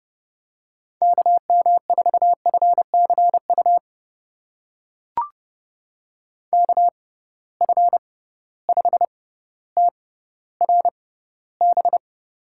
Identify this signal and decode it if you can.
Morse code, heard as a keyed tone, about 31 wpm, spaced as for 15 wpm, 700 Hz. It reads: KM4FCU KF5TRB
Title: USA Call Signs - 6 Characters - 6x Character Spacing - 30wpm